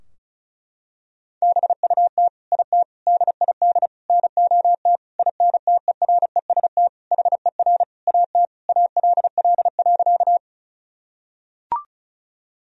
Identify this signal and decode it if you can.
Morse code, heard as a keyed tone, about 36 wpm, 700 Hz. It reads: BUT IT DID NOT INTEREST HER AT ALL.
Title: Sentences from Top 1000 Words - Review Part 2 - 35wpm